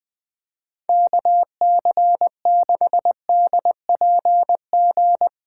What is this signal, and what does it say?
Morse code, heard as a keyed tone, about 20 wpm, 700 Hz. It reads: KC6DPG
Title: USA Callsign DE Callsign - 6 Characters - 20wpm